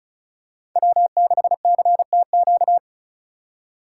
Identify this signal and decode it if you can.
Morse code, heard as a keyed tone, about 36 wpm, 700 Hz. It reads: W6CTQ
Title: USA Call Signs - 5 Characters - 4x Character Spacing - 35wpm